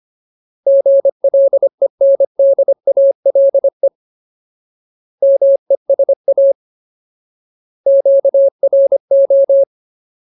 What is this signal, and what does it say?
Morse code, heard as a keyed tone, about 25 wpm, 550 Hz. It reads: GLENDALE MESA QRO
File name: Ultimate Word List / 25WPM Ultimate